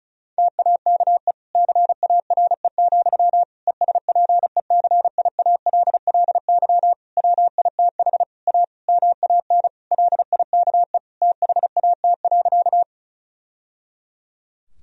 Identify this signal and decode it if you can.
Morse code, heard as a keyed tone, about 36 wpm, 700 Hz. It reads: TAKE CARE, ESPECIALLY WITH A MAN LIKE THAT.
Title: Sentences from Top 1000 Words - Review Part 4 - 35wpm